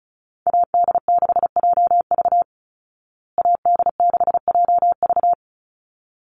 Morse code, heard as a keyed tone, about 34 wpm, 700 Hz. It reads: AD6JV AD6JV
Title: Call Signs Repeated 3x - 35wpm